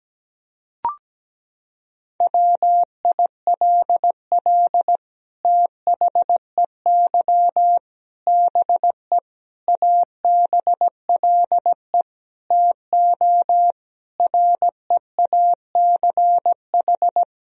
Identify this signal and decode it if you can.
Morse code, heard as a keyed tone, about 17 wpm, 700 Hz. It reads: WILL THEY BE ABLE TO REACH
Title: Sentences from Top 1000 Words - Review Part 2 - 17wpm